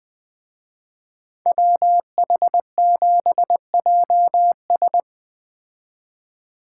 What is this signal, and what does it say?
Morse code, heard as a keyed tone, about 20 wpm, 700 Hz. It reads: WH7JS
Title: Speed Racing - USA 2x2 Call Signs 17wpm